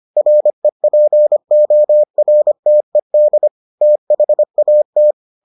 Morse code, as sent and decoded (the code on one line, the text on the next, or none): .-. .--. --- .-. - . -.. / - .... .- -
REPORTED THAT